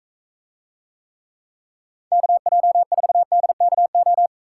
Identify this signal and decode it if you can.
Morse code, heard as a keyed tone, about 41 wpm, 700 Hz. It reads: KJ4DKO